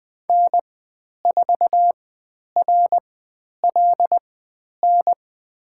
Morse code, heard as a keyed tone, about 20 wpm, spaced as for 10 wpm, 700 Hz. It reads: N4RLN
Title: 109 Lesson 18g - Letter m - Call Signs 20wpm